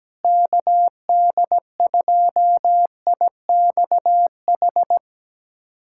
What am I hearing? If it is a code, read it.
Morse code, heard as a keyed tone, about 17 wpm, 700 Hz. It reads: KD2IXH
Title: USA Callsign DE Callsign - 6 Characters - 17wpm